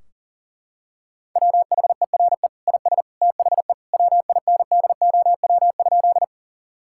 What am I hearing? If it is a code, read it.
Morse code, heard as a keyed tone, about 40 wpm, 700 Hz. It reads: WHERE IS THE WINDOW?
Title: Sentences from Top 1600 Words 40wpm